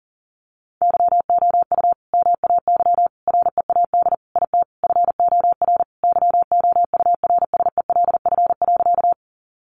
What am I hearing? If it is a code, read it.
Morse code, heard as a keyed tone, about 40 wpm, 700 Hz. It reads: YOU MAY READ IT FOR YOURSELF.
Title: Sentences from Top 600 Words 40wpm